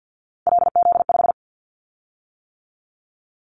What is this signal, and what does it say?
Morse code, heard as a keyed tone, about 50 wpm, 700 Hz. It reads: LZ5
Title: Callsign Prefixes for Top 50 DXCC - 50wpm